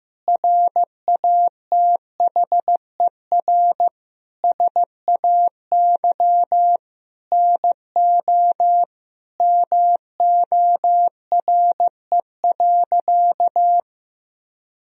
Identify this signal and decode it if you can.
Morse code, heard as a keyed tone, about 15 wpm, 700 Hz. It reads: RATHER SAY NO MORE.
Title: Sentences from Top 1000 Words - Review Part 4 - 15wpm